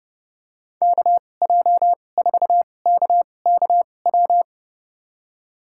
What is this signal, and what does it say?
Morse code, heard as a keyed tone, about 32 wpm, spaced as for 22 wpm, 700 Hz. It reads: KJ4KKW
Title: USA Call Signs - 6 Characters - 2x Character Spacing - 30wpm